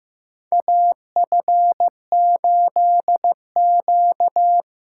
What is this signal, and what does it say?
Morse code, heard as a keyed tone, about 15 wpm, 700 Hz. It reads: AF8Q
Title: USA Call Signs - 4x Character Spacing - 15wpm